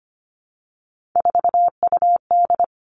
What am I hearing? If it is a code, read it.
Morse code, heard as a keyed tone, about 25 wpm, 700 Hz. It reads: 4UD
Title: All Cut Numbers - 3-Digit 25wpm